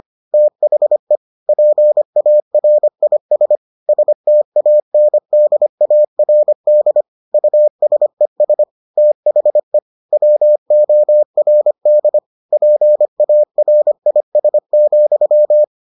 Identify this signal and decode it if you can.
Morse code, heard as a keyed tone, about 25 wpm, 600 Hz. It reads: THE PARIS STANDARD USES THE WORD PARIS,